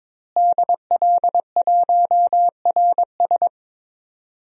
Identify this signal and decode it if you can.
Morse code, heard as a keyed tone, about 21 wpm, 700 Hz. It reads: DL1RS
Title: DXCC-100 Call Signs - 5 Characters - Repeated 3x - 22wpm